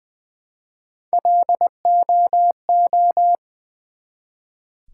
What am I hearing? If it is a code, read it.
Morse code, heard as a keyed tone, about 20 wpm, 700 Hz. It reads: LOO
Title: Build A Word - 1 Letter At A Time - 3 to 5-Letter Words - 20wpm